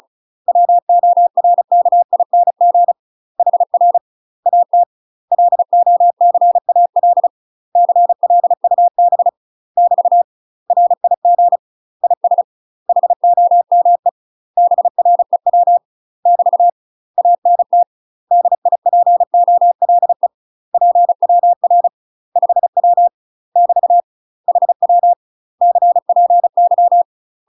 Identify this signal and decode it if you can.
Morse code, heard as a keyed tone, about 35 wpm, 700 Hz. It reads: WORKING HR AT LOCAL CLUB <BT> RIG IS HOME BREW <BT> ANT DIPOLE PWR 5W <BT> HW CPY
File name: UK MASTER QSOs TRAINING/35WPM